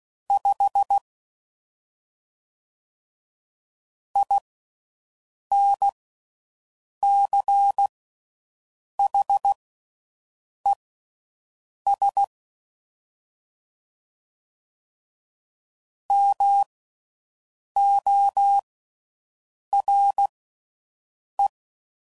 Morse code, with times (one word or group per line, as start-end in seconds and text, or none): 0.3-1.0 5
4.2-12.2 INCHES
16.1-21.5 MORE